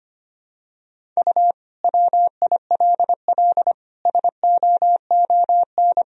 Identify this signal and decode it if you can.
Morse code, heard as a keyed tone, about 25 wpm, 700 Hz. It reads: U WILL SOON
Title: Sentences from Top 400 Words and Abbreviations 25wpm